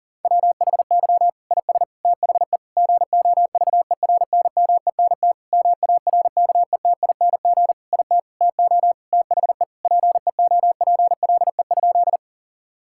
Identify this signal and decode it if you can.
Morse code, heard as a keyed tone, about 40 wpm, 700 Hz. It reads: WHY IS THE GOVERNMENT MARKETING IT TO THE PEOPLE?
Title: Sentences from Top 1000 Words - Review Part 5 - 40wpm